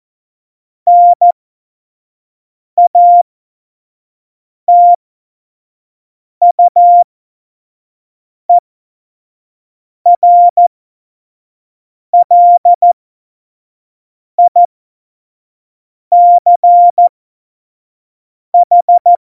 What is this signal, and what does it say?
Morse code, heard as a keyed tone, about 14 wpm, spaced as for 5 wpm, 700 Hz. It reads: NATUERLICH
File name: Morsekurs / Witze / Witz 30. April 2019, 25 F70